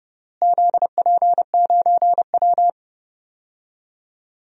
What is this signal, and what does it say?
Morse code, heard as a keyed tone, about 30 wpm, 700 Hz. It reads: ZP9W
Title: DXCC-100 Call Signs - 4 Characters - Repeated 3x - 30wpm